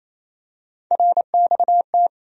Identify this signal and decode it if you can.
Morse code, heard as a keyed tone, about 28 wpm, 700 Hz. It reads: RXT
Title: USA Call Sign Postfixes - 3 Letters 28wpm